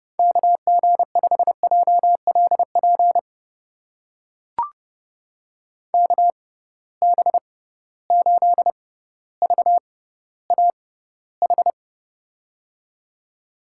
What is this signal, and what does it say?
Morse code, heard as a keyed tone, about 30 wpm, spaced as for 15 wpm, 700 Hz. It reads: KG5JLP KB8VAH
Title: USA Call Signs - 6 Characters - 6x Character Spacing - 30wpm